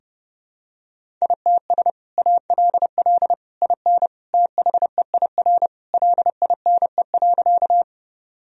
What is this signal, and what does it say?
Morse code, heard as a keyed tone, about 30 wpm, 700 Hz. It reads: ITS ALL IN THEIR LINE.
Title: Sentences from Top 1000 Words - Review Part 1 - 30wpm